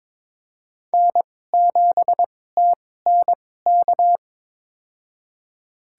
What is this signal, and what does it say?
Morse code, heard as a keyed tone, about 22 wpm, spaced as for 16 wpm, 700 Hz. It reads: N7TNK
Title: USA Call Signs - 5 Characters - 2x Character Spacing - 22wpm